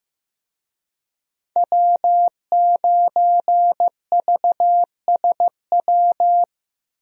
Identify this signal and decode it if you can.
Morse code, heard as a keyed tone, about 15 wpm, 700 Hz. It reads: W9VSW
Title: USA Call Signs - 5 Characters - Repeated 3x - 15wpm